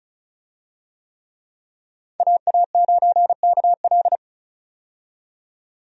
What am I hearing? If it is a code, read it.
Morse code, heard as a keyed tone, about 36 wpm, 700 Hz. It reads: AA9KL
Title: USA Call Signs - 5 Characters - 35wpm